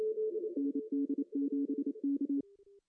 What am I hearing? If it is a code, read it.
Morse code, heard as a keyed tone, about 28 wpm, 280 Hz. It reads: ND7K
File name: CW / cqww-cw-k5tr